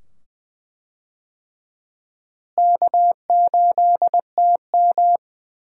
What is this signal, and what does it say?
Morse code, heard as a keyed tone, about 20 wpm, 700 Hz. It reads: K8TM
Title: USA Call Signs - 4 Characters - Repeated 2x - 20wpm